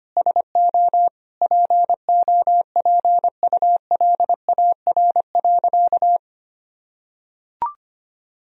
Morse code, heard as a keyed tone, about 25 wpm, 700 Hz. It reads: SO POPULAR.
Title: Sentences from Top 1000 Words - Review Part 3 - 25wpm